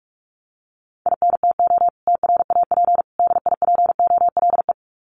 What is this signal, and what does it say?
Morse code, heard as a keyed tone, about 44 wpm, 700 Hz. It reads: INTO TRAP DIPOLE
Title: Common QSO Phrases 45wpm